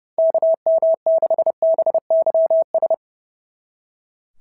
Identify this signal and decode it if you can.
Morse code, heard as a keyed tone, about 28 wpm, 690 Hz. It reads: KM6BYS
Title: SOTA QSO Practice 30wpm